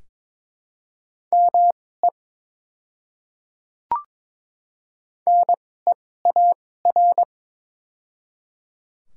Morse code, heard as a keyed tone, about 22 wpm, spaced as for 16 wpm, 700 Hz. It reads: ME NEAR
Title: 2x Character Spacing - Top 300 Words - 22wpm